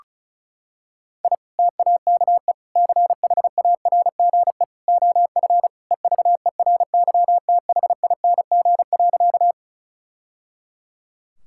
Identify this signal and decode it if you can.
Morse code, heard as a keyed tone, about 34 wpm, 700 Hz. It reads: I TAKE CHARGE OF EVERYTHING.
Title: Sentences from Top 1000 Words - Review Part 6 - 35wpm